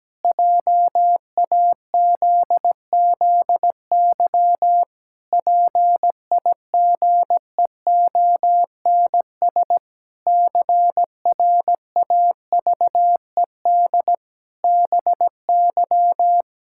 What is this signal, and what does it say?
Morse code, heard as a keyed tone, about 17 wpm, 700 Hz. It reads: JAZZY PIGEONS CRAVED BY
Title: Pangram Challenge v2 - 17wpm